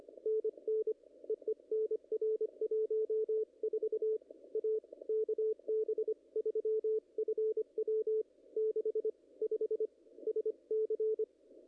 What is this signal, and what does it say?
Morse code, heard as a keyed tone, about 23 wpm, 410 Hz. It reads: NN EE NR 14 A KB3FW 65 SC
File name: contests / ss-cw-k5tr-op-n5rz -0100